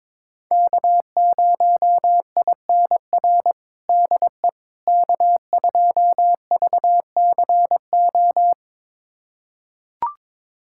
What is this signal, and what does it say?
Morse code, heard as a keyed tone, about 22 wpm, 700 Hz. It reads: K0INR DE K2VCO
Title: USA Callsign DE Callsign - 5 Characters - 22wpm